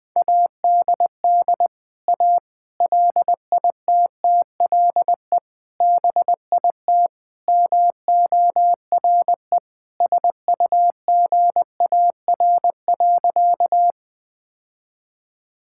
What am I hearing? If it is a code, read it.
Morse code, heard as a keyed tone, about 20 wpm, 700 Hz. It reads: ADD A LITTLE BIT MORE SUGAR.
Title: Sentences with 4-Word Phrases - Spoken Beginning - 20wpm